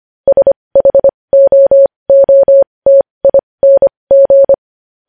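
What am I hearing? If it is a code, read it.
Morse code, heard as a keyed tone, about 25 wpm, spaced as for 20 wpm, 550 Hz. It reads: SHOOTING
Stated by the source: Morse practice sending news headlines at 20 wpm.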